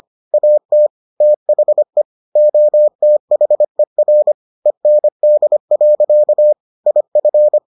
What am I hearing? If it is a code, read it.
Morse code, heard as a keyed tone, about 25 wpm, 600 Hz. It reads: AT THE OTHER END. IF